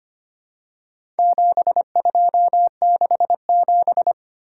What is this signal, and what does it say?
Morse code, heard as a keyed tone, about 25 wpm, 700 Hz. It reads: 7267
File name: practice / 25wpm / Four Letter-Number Combinations - 25wpm